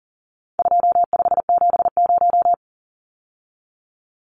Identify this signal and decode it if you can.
Morse code, heard as a keyed tone, about 40 wpm, 700 Hz. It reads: 2570
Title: Numbers - 4-Digit 40wpm